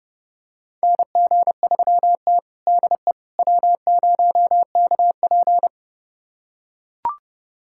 Morse code, heard as a keyed tone, about 30 wpm, 700 Hz. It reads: NG3T DE W0KP
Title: USA Callsign DE Callsign - 4 Characters - 30wpm